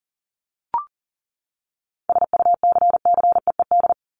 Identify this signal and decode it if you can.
Morse code, heard as a keyed tone, about 38 wpm, 700 Hz. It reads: SUCCEED
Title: Top 1501-1750 Words 40wpm